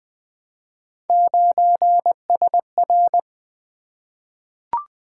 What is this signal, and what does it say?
Morse code, heard as a keyed tone, about 20 wpm, 700 Hz. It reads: 9SR
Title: Three Letter-Number Combinations - 20wpm